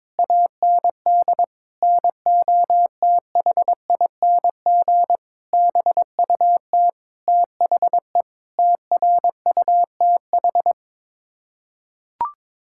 Morse code, heard as a keyed tone, about 21 wpm, 700 Hz. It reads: AND NOTHING BUT THE TRUTH
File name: practice / 22wpm / Sets of 8 Words - 22wpm